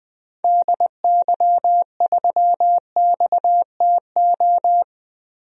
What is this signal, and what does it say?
Morse code, heard as a keyed tone, about 21 wpm, 700 Hz. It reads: DY3XTO
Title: DXCC-100 Call Signs - 6 Characters - Repeated 3x - 20wpm